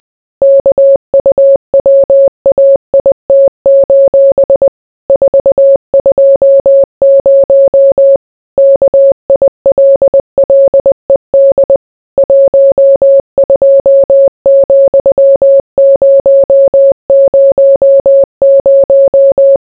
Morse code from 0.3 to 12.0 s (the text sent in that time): KUWAIT: 420 KILLED